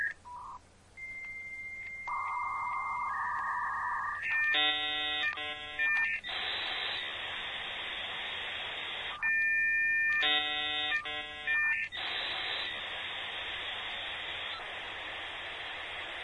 Dial-up modem handshake.